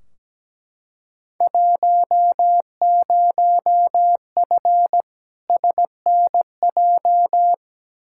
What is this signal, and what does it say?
Morse code, heard as a keyed tone, about 17 wpm, 700 Hz. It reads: 10F SNJ
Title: Field Day Exchanges Encore - 17wpm